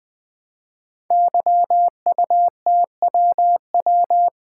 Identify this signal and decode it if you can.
Morse code, heard as a keyed tone, about 20 wpm, 700 Hz. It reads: YUTWW
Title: Code Groups - 5 Letters - 20wpm